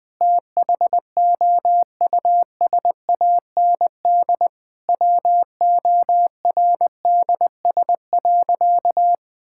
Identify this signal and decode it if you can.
Morse code, heard as a keyed tone, about 21 wpm, 700 Hz. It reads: THOUSAND WORDS.
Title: Sentences with 7-Word Phrases - Spoken Beginning - 20wpm